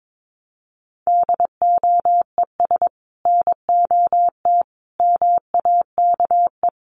Morse code, heard as a keyed tone, about 22 wpm, 700 Hz. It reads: DOES NOT MAKE
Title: Sets of 5 Words Encore2 22wpm